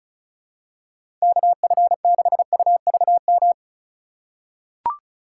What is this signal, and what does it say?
Morse code, heard as a keyed tone, about 36 wpm, 700 Hz. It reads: KF6UVM
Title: USA Call Signs - 2x Character Spacing - 35wpm